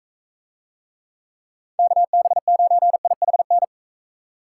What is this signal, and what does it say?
Morse code, heard as a keyed tone, about 44 wpm, 700 Hz. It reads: KB9IHN